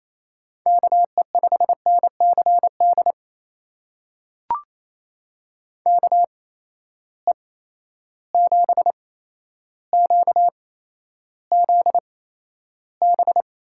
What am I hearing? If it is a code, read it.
Morse code, heard as a keyed tone, about 28 wpm, spaced as for 12 wpm, 700 Hz. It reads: KE5NCD KE7QZB
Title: USA Call Signs - 6 Characters - 8x Character Spacing - 28wpm